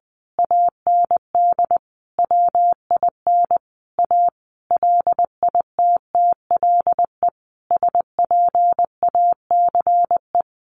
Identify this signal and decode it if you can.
Morse code, heard as a keyed tone, about 20 wpm, 700 Hz. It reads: AND WIN A LITTLE SPACE
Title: Sentences from Top 1500 Words 20wpm